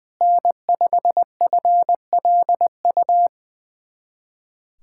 Morse code, heard as a keyed tone, about 20 wpm, 700 Hz. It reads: N5FLU
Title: USA Callsign DE Callsign - 5 Characters - 20wpm